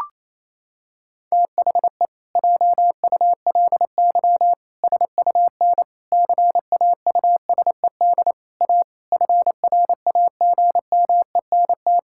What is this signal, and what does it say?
Morse code, heard as a keyed tone, about 28 wpm, 700 Hz. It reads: THE JULY SUN CAUSED A FRAGMENT
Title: Pangram Challenge v2 - 28wpm